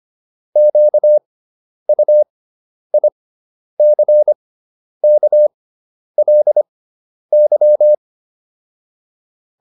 Morse code, heard as a keyed tone, about 25 wpm, spaced as for 10 wpm, 600 Hz. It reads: QUICKLY